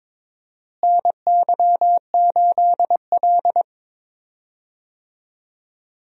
Morse code, heard as a keyed tone, about 22 wpm, 700 Hz. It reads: NY8L
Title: USA Call Signs - 4 Characters - Repeated 2x - 22wpm